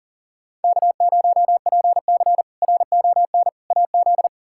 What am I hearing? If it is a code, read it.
Morse code, heard as a keyed tone, about 37 wpm, 700 Hz. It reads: K0PC RON AZ
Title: K1USN SST QSO Practice - 40wpm